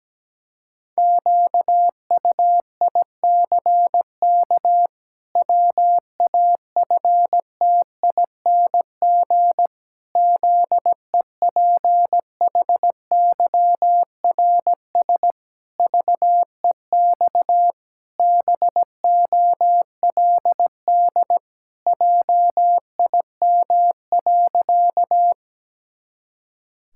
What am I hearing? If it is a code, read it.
Morse code, heard as a keyed tone, about 17 wpm, 700 Hz. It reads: QUICK WAFTING ZEPHYRS VEX BOLD JIM.
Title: Pangram Challenge v2 - 17wpm